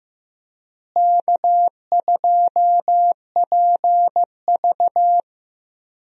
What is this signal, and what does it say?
Morse code, heard as a keyed tone, about 15 wpm, 700 Hz. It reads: K2PV
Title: USA Call Signs - 4 Characters - Repeated 3x - 15wpm